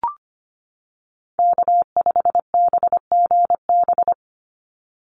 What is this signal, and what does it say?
Morse code, heard as a keyed tone, about 25 wpm, 700 Hz. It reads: K5BGB
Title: Call Signs - 25wpm